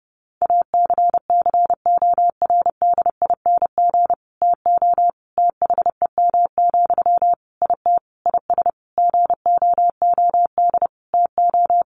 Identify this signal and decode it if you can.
Morse code, heard as a keyed tone, about 30 wpm, 700 Hz. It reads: ACCORDING TO THEM, IT IS GOOD TO
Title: Sentences from Top 800 Words 30wpm